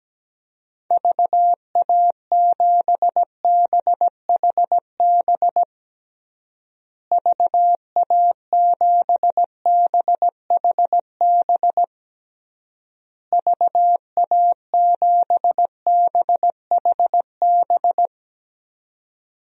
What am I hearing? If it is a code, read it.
Morse code, heard as a keyed tone, about 17 wpm, 700 Hz. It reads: VA7BHB VA7BHB VA7BHB
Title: DXCC-100 Call Signs - 6 Characters - Repeated 3x - 17wpm